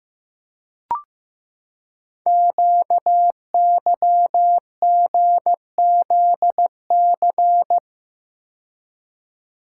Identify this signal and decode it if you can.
Morse code, heard as a keyed tone, about 15 wpm, 700 Hz. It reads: QYGZC
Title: Code Groups - 5 Letters - 15wpm